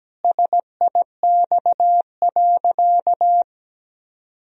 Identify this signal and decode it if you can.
Morse code, heard as a keyed tone, about 17 wpm, 700 Hz. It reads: SIX.
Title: Sentences from Top 1000 Words - Review Part 2 - 17wpm